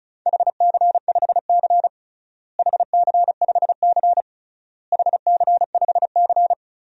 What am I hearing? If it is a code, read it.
Morse code, heard as a keyed tone, about 36 wpm, 700 Hz. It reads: HC5C HC5C HC5C
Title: DXCC-100 Call Signs - 4 Characters - Repeated 3x - 35wpm